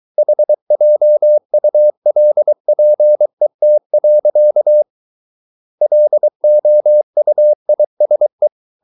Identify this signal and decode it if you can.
Morse code, heard as a keyed tone, about 23 wpm, 600 Hz. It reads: HJULPET. LOUISE
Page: Ferie 23 wpm